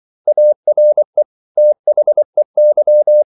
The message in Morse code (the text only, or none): ARE THEY